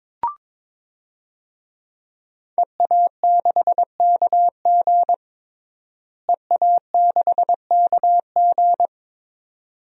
Morse code, heard as a keyed tone, about 22 wpm, 700 Hz. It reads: EA6KG EA6KG
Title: DXCC-100 Call Signs - 5 Characters - Repeated 2x - 22wpm